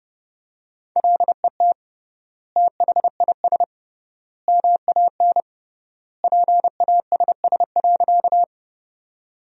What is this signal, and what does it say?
Morse code, heard as a keyed tone, about 28 wpm, 700 Hz. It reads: LET THIS MAN PASS.
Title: 3x Word-Spacing - Sentences from Top 400 Words - 30wpm